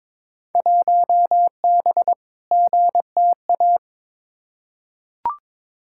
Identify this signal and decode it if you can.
Morse code, heard as a keyed tone, about 22 wpm, 700 Hz. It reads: 1B GTA
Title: Field Day Exchanges Encore - 22wpm